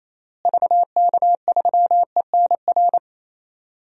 Morse code, heard as a keyed tone, about 28 wpm, 700 Hz. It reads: VK3ENR
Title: DXCC-100 Call Signs - 6 Characters - Repeated 3x - 28wpm